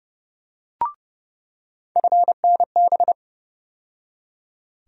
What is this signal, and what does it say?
Morse code, heard as a keyed tone, about 29 wpm, 700 Hz. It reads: FNB
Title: USA Call Sign Postfixes - 3 Letters 30wpm